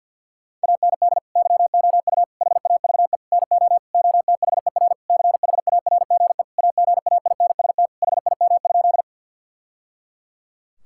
Morse code, heard as a keyed tone, about 46 wpm, 700 Hz. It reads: AND YOU HAVE NO OTHER CHARGE AGAINST HIM?
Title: Sentences from Top 1000 Words - Review Part 6 - 50wpm